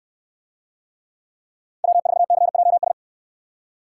Morse code, heard as a keyed tone, about 69 wpm, 690 Hz. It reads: W4CJS